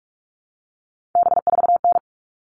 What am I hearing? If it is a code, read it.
Morse code, heard as a keyed tone, about 44 wpm, 700 Hz. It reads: B4N
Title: All Cut Numbers - 3-Digit 45wpm